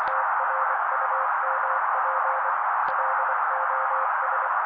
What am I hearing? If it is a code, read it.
Morse code, heard as a keyed tone, about 24 wpm, 530 Hz. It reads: TRUMP LOS
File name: daily / 23wpm noise